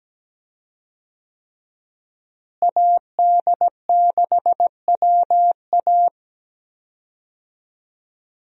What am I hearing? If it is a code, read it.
Morse code, heard as a keyed tone, about 17 wpm, 700 Hz. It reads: AD6WA
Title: USA Call Signs - 5 Characters - Repeated 3x - 17wpm